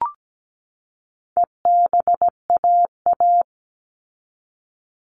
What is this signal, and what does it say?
Morse code, heard as a keyed tone, about 17 wpm, 700 Hz. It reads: EBAA
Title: All Cut Numbers - 4-Digit 17wpm